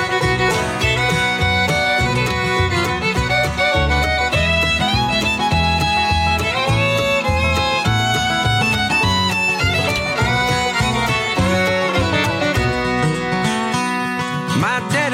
Dial-up modem handshake.